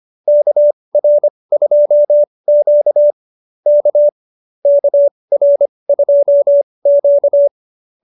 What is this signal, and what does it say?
Morse code, heard as a keyed tone, about 25 wpm, spaced as for 20 wpm, 590 Hz. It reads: KR2Q K KR2Q
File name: QSO220 20